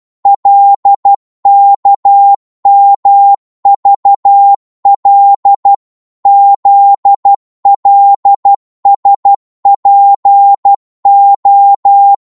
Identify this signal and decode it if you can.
Morse code, heard as a keyed tone, about 12 wpm, 800 Hz. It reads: LKMVL ZLSPO
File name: Morse / ltr-12wpm